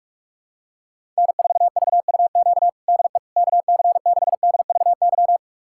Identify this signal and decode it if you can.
Morse code, heard as a keyed tone, about 45 wpm, 690 Hz. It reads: N4VUQ DE KC6DVY
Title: USA Callsign DE Callsign 45wpm